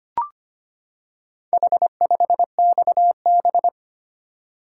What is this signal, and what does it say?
Morse code, heard as a keyed tone, about 25 wpm, 700 Hz. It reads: H5XB
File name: practice / 25wpm / Four Letter-Number Combinations - 25wpm